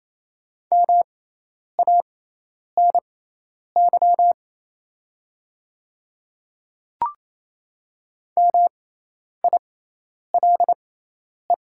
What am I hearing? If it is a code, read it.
Morse code, heard as a keyed tone, about 27 wpm, spaced as for 10 wpm, 700 Hz. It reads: MANY MILE
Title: 6x Character Spacing - Top 300 Words - 28wpm